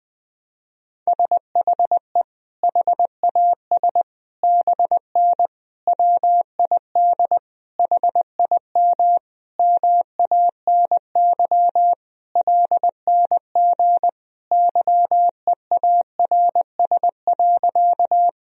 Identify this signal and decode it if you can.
Morse code, heard as a keyed tone, about 21 wpm, 700 Hz. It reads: SHE HAS BN WID HIM MANY LNG YEARS.
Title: Sentences from Top 200 Words and Abbreviations 20wpm